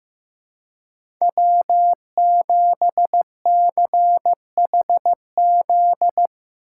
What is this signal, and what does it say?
Morse code, heard as a keyed tone, about 15 wpm, 700 Hz. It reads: W7CHZ
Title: USA Call Signs - 5 Characters - 2x Character Spacing - 15wpm